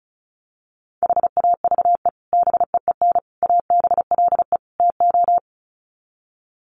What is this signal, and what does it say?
Morse code, heard as a keyed tone, about 34 wpm, 700 Hz. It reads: HAVE BEEN ABLE TO
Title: Sets of 4 Words 35wpm